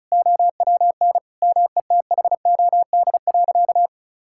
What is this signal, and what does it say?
Morse code, heard as a keyed tone, about 36 wpm, 700 Hz. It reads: OWN METHOD.
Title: Sentences from Top 1000 Words - Review Part 6 - 35wpm